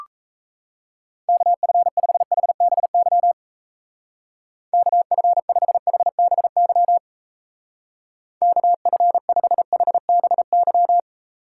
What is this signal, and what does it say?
Morse code, heard as a keyed tone, about 33 wpm, 700 Hz. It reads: KF5HBY KF5HBY KF5HBY
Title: Speed Racing - USA Call Signs 28wpm